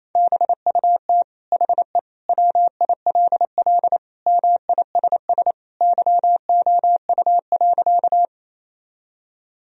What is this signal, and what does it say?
Morse code, heard as a keyed tone, about 28 wpm, 700 Hz. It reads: BUT HE WILL MISS YOU.
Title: Sentences from Top 1000 Words - Review Part 3 - 28wpm